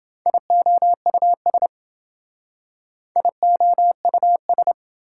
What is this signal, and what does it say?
Morse code, heard as a keyed tone, about 29 wpm, 700 Hz. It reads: IOUS IOUS